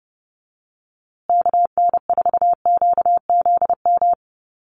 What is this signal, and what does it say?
Morse code, heard as a keyed tone, about 30 wpm, 700 Hz. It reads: KN4QZM
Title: USA Call Signs Repeated 3x 30wpm